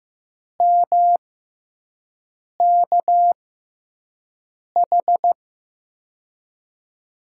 Morse code, heard as a keyed tone, about 15 wpm, spaced as for 5 wpm, 700 Hz. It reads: MKH
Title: USA Call Signs - 6 Characters - 6x Character Spacing - 15wpm